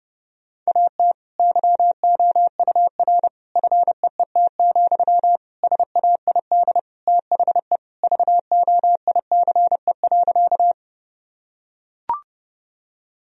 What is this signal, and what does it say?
Morse code, heard as a keyed tone, about 30 wpm, 700 Hz. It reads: AT YOUR FEET, SAID THE VOICE.
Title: Sentences from Top 1000 Words - Review Part 1 - 30wpm